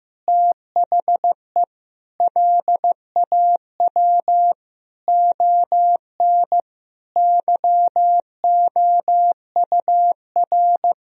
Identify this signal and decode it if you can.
Morse code, heard as a keyed tone, about 15 wpm, 700 Hz. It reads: THE LAW ON YOUR
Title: Sentences from Top 1000 Words - Review Part 3 - 15wpm